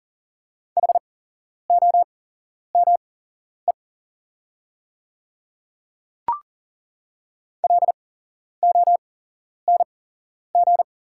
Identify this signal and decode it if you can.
Morse code, heard as a keyed tone, about 40 wpm, spaced as for 11 wpm, 700 Hz. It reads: HOME LONG
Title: 8x Character Spacing - Top 200 Words - 40wpm